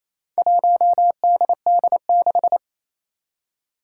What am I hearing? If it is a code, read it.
Morse code, heard as a keyed tone, about 28 wpm, 700 Hz. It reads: 1DD6
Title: Four Challenging Letter-Number Combinations 28wpm